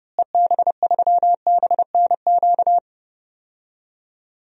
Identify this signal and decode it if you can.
Morse code, heard as a keyed tone, about 30 wpm, 700 Hz. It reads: EB3BNQ
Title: DXCC-100 Call Signs - 6 Characters - Repeated 2x - 30wpm